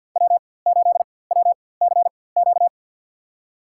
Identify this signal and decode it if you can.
Morse code, heard as a keyed tone, about 49 wpm, spaced as for 23 wpm, 700 Hz. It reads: W8WCQ
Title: USA Call Signs - 5 Characters - 4x Character Spacing - 50wpm